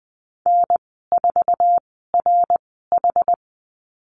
Morse code, heard as a keyed tone, about 20 wpm, spaced as for 15 wpm, 700 Hz. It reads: N4RH
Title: 062 Lesson 11g - Letter h - Call Signs 20wpm